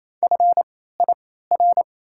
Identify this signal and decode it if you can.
Morse code, heard as a keyed tone, about 28 wpm, spaced as for 16 wpm, 690 Hz. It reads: FIR
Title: 3x Character Spacing - Top 500 Words - 28wpm